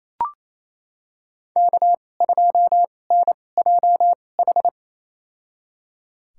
Morse code, heard as a keyed tone, about 28 wpm, spaced as for 20 wpm, 700 Hz. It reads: K2NJH
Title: USA Call Signs - 2x Character Spacing - 28wpm